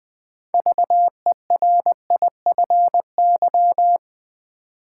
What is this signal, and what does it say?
Morse code, heard as a keyed tone, about 20 wpm, 700 Hz. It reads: VERIFY
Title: Build A Word - 3 Letters At A Time - 6-Letter Words - 20wpm